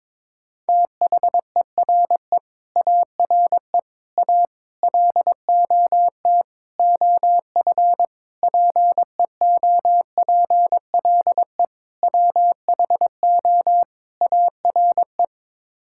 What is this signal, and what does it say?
Morse code, heard as a keyed tone, about 21 wpm, 700 Hz. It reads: THERE ARE A LOT OF PEOPLE WHO ARE
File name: practice / 22wpm / Sets of 8 Words - 22wpm